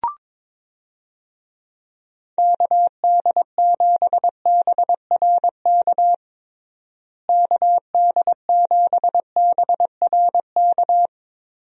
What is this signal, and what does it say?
Morse code, heard as a keyed tone, about 22 wpm, 700 Hz. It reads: KD7BRK KD7BRK
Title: USA Call Signs - 6 Characters - Repeated 3x - 22wpm